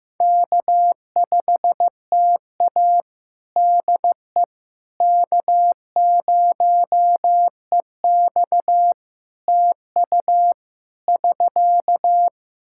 Morse code, heard as a keyed tone, about 15 wpm, 700 Hz. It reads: K5TA DE K0EX TU <SK>
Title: Practice QSO - Exchange 4 - Ending - 15wpm